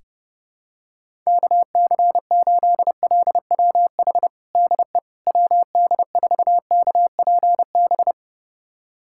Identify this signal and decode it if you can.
Morse code, heard as a keyed tone, about 30 wpm, 700 Hz. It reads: KC8LWH DE WD4KPB
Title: USA Callsign DE Callsign 30wpm